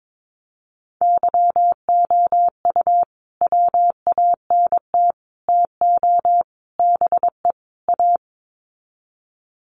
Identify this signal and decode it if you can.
Morse code, heard as a keyed tone, about 22 wpm, 700 Hz. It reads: YOU WANT TO BE A
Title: Sets of 5 Words Encore2 22wpm